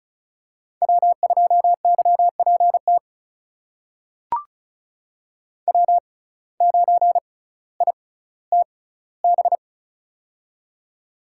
Morse code, heard as a keyed tone, about 36 wpm, spaced as for 18 wpm, 700 Hz. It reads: W2YPT W9ITB
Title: USA Call Signs - 6x Character Spacing - 35wpm